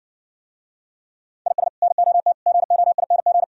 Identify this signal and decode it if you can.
Morse code, heard as a keyed tone, about 69 wpm, 700 Hz. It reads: IS NOT GOING